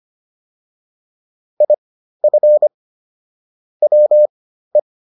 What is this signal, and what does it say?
Morse code, heard as a keyed tone, about 25 wpm, spaced as for 13 wpm, 600 Hz. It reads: IF WE